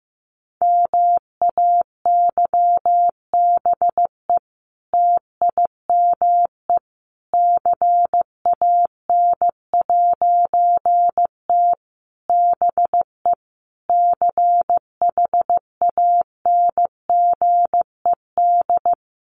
Morse code, heard as a keyed tone, about 15 wpm, 700 Hz. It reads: MAYBE TIME CAN'T BE CHANGED
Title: A Sound Of Thunder 15wpm